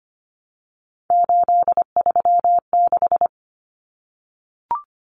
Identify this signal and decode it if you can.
Morse code, heard as a keyed tone, about 25 wpm, 690 Hz. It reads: 836
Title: Numbers - 3-Digit 25wpm